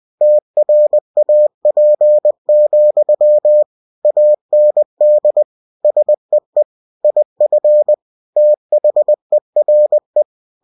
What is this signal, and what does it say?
Morse code, heard as a keyed tone, about 20 wpm, 600 Hz. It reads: TRAP, AND SEE IF THERE
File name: cw-books / winnie-the-pooh / winnie-20wpm-page-0041